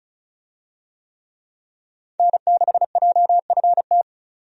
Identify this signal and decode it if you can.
Morse code, heard as a keyed tone, about 36 wpm, 700 Hz. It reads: N6JFT
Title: USA Call Signs - 5 Characters - 35wpm